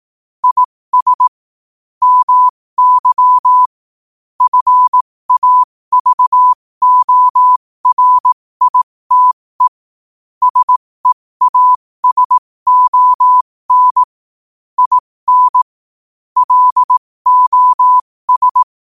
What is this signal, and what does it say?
Morse code, heard as a keyed tone, about 18 wpm, spaced as for 15 wpm, 1000 Hz. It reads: IS MY FAVORITE SEASON IN LOS
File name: qotdfiles / 15wpm